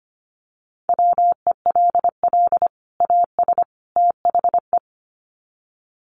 Morse code, heard as a keyed tone, about 25 wpm, 700 Hz. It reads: WELL AS THE
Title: Sets of 3 Words 25wpm